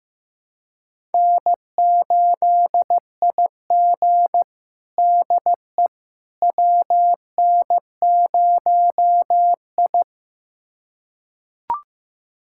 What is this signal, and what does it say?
Morse code, heard as a keyed tone, about 15 wpm, 700 Hz. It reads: N8IG DE WN0I
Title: USA Callsign DE Callsign - 4 Characters - 15wpm